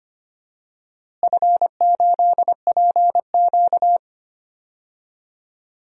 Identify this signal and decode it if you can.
Morse code, heard as a keyed tone, about 25 wpm, 700 Hz. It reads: F8PQ
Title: DXCC-100 Call Signs - 4 Characters - Repeated 3x - 25wpm